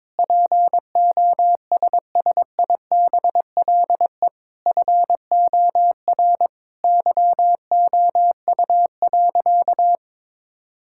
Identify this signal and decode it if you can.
Morse code, heard as a keyed tone, about 22 wpm, 700 Hz. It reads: POSSIBLE FOR YOU.
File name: Sentences with 7-Word Phrases - 22wpm